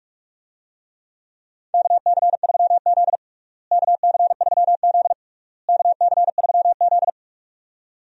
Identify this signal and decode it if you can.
Morse code, heard as a keyed tone, about 45 wpm, 700 Hz. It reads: KC3Z KC3Z KC3Z
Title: USA Call Signs - 4 Characters - Repeated 3x - 45wpm